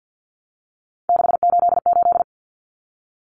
Morse code, heard as a keyed tone, about 50 wpm, 700 Hz. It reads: 688
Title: Numbers - 3-Digit 50wpm